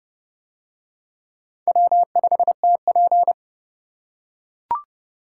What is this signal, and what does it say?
Morse code, heard as a keyed tone, about 30 wpm, 700 Hz. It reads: W5TP